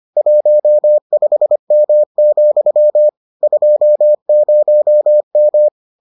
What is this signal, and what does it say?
Morse code, heard as a keyed tone, about 25 wpm, 600 Hz. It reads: 15M, 20M